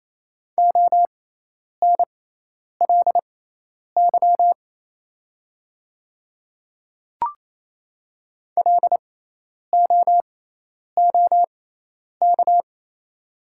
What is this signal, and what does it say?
Morse code, heard as a keyed tone, about 28 wpm, spaced as for 10 wpm, 700 Hz. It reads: ONLY LOOK
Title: 6x Character Spacing - Top 500 Words - 28wpm